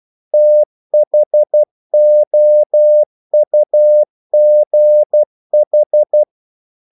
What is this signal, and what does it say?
Morse code, heard as a keyed tone, about 12 wpm, 600 Hz. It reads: THOUGH
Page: English quotations in CW (telegraphy - morse)